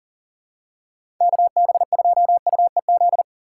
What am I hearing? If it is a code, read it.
Morse code, heard as a keyed tone, about 40 wpm, 700 Hz. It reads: KB2UEZ
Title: USA Call Signs - 6 Characters - 6x Character Spacing - 40wpm